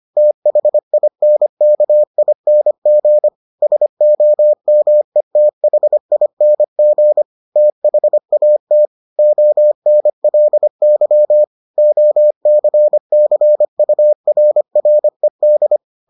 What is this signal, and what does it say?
Morse code, heard as a keyed tone, about 25 wpm, 600 Hz. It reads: THINKING SOMETHING THAT ONLY OCCURRED